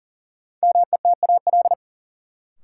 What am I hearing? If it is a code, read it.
Morse code, heard as a keyed tone, about 37 wpm, 700 Hz. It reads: METAL
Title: Sentences from Top 1000 Words - Review Part 6 - New Word Drill - 40wpm